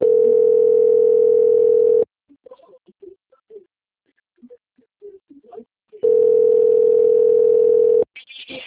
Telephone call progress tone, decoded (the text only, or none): ringback tone